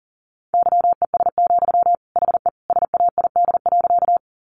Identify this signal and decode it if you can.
Morse code, heard as a keyed tone, about 40 wpm, 700 Hz. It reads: YES, HE SAID.
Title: Sentences from Top 500 Words 40wpm